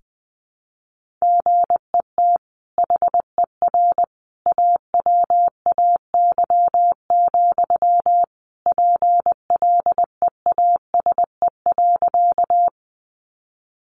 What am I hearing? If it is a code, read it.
Morse code, heard as a keyed tone, about 20 wpm, 700 Hz. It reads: GET HER AWAY, PLEASE.
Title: Sentences from Top 1500 Words 20wpm